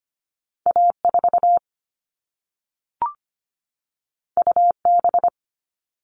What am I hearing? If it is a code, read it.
Morse code, heard as a keyed tone, about 25 wpm, 700 Hz. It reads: A4 UB
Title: All Cut Numbers - 2-Digit 25wpm